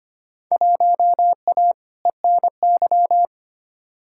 Morse code, heard as a keyed tone, about 25 wpm, 700 Hz. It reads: 1A ENY
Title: Field Day Exchanges Encore - 25wpm